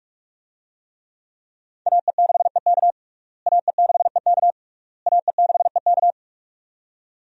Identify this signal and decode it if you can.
Morse code, heard as a keyed tone, about 45 wpm, 700 Hz. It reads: AE6EK AE6EK AE6EK
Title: USA Call Signs - 5 Characters - Repeated 3x - 45wpm